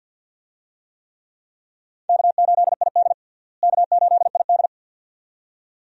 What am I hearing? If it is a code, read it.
Morse code, heard as a keyed tone, about 51 wpm, 700 Hz. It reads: K8ID K8ID
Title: USA Call Signs - 4 Characters - Repeated 2x - 50wpm